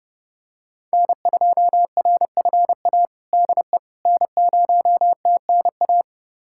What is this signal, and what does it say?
Morse code, heard as a keyed tone, about 32 wpm, 700 Hz. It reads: N2RFA DE N0TNA
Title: USA Callsign DE Callsign - 5 Characters - 30wpm